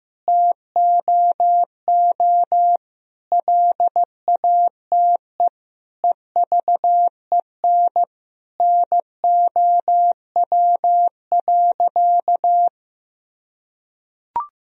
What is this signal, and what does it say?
Morse code, heard as a keyed tone, about 15 wpm, 700 Hz. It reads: TOO LATE EVEN NOW.
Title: Sentences from Top 1000 Words - Review Part 1 - 15wpm